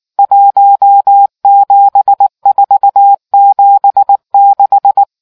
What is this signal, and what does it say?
Morse code, heard as a keyed tone, about 19 wpm, 800 Hz. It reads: 17476